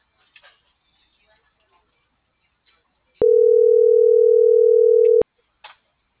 Ringback tone (North America), 440 + 480 Hz.